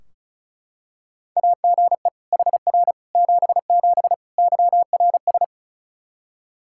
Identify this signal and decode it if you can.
Morse code, heard as a keyed tone, about 34 wpm, 700 Hz. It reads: AGE HR 77 YRS
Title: Practice QSO - Exchange 3 - More Info - 35wpm